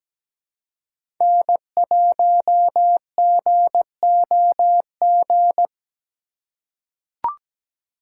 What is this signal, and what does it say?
Morse code, heard as a keyed tone, about 17 wpm, 700 Hz. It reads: N1GOG
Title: USA Call Signs - 5 Characters - 2x Character Spacing - 17wpm